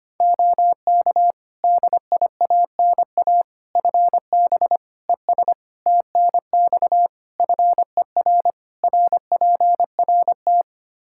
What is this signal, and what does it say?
Morse code, heard as a keyed tone, about 25 wpm, 700 Hz. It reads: OK DIANA FB ES TNX FER RPRT
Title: Practice QSO - Exchange 2 - Info - 25wpm